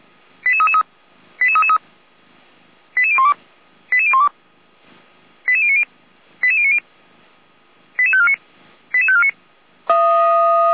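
Five-tone selective call (ZVEI), heard as quick runs of five tones, tones of about 80 ms. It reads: 88333 88333 88012 88012 88090 88090 88549 88549
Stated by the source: ZVEI five-tone call, code 88341, received by SDR; Feuerwehr Orlamünde.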